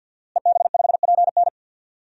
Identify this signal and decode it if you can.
Morse code, heard as a keyed tone, about 52 wpm, 700 Hz. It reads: EB5PN
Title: DXCC-100 Call Signs - 5 Characters - Repeated 2x - 50wpm